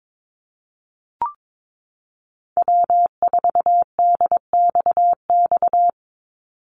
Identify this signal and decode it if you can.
Morse code, heard as a keyed tone, about 22 wpm, 700 Hz. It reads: W4DXX
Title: Call Signs Repeated 2x - 22wpm